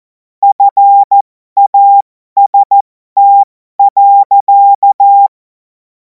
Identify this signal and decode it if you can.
Morse code, heard as a keyed tone, about 14 wpm, spaced as for 12 wpm, 800 Hz. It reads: FAST.
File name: Morsekurs / Witze